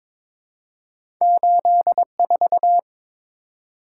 Morse code, heard as a keyed tone, about 22 wpm, 700 Hz. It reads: 84
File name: practice / 22wpm / Two Letter-Number Combinations - 22wpm